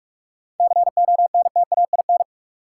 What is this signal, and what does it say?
Morse code, heard as a keyed tone, about 44 wpm, 700 Hz. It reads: CONTAIN
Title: Sentences from Top 500 Words - New Word Drill - 45wpm